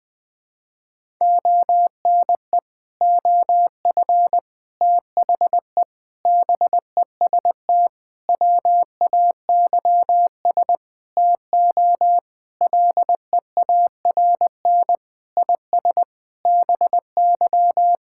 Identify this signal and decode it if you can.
Morse code, heard as a keyed tone, about 21 wpm, 700 Hz. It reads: ONE OF THE BEST WAYS TO LEARN IS BY
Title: Sentences with 6-Word Phrases - Spoken Beginning - 20wpm